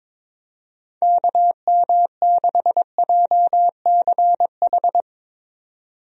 Morse code, heard as a keyed tone, about 22 wpm, 700 Hz. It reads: KM6JCH
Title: USA Call Signs - 6 Characters - 22wpm